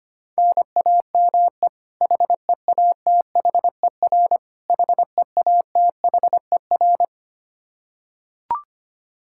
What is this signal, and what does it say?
Morse code, heard as a keyed tone, about 25 wpm, 700 Hz. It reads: NAME HEATHER HEATHER
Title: K1USN SST QSO Practice 25wpm